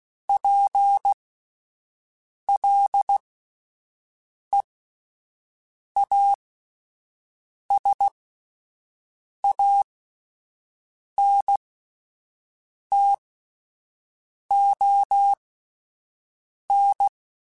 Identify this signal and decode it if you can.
Morse code, heard as a keyed tone, about 16 wpm, spaced as for 5 wpm, 800 Hz. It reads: PLEASANTON